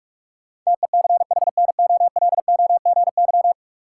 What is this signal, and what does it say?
Morse code, heard as a keyed tone, about 44 wpm, 700 Hz. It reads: TECHNOLOGY
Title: Sentences from Top 1200 Words - New Word Drill Alt - 45wpm